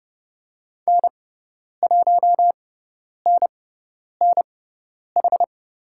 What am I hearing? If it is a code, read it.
Morse code, heard as a keyed tone, about 30 wpm, spaced as for 10 wpm, 700 Hz. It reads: N1NNH
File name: Lesson 18g - Letter m - Call Signs 30wpm